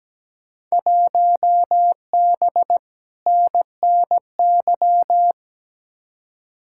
Morse code, heard as a keyed tone, about 17 wpm, 700 Hz. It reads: 1B NNY
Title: Field Day Exchanges Encore - 17wpm